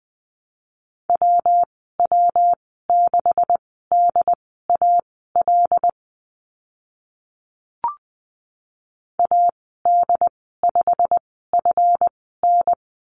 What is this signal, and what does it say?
Morse code, heard as a keyed tone, about 20 wpm, spaced as for 14 wpm, 700 Hz. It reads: WW6DAL AD5FN
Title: 154 Lesson 25g - Letter p - Call Signs 20wpm